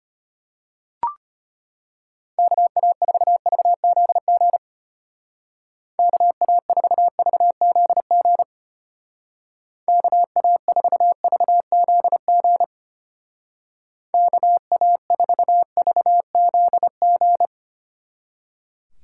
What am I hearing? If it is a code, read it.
Morse code, heard as a keyed tone, about 32 wpm, 700 Hz. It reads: KA4VZG KA4VZG KA4VZG KA4VZG